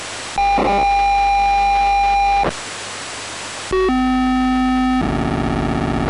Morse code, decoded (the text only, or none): none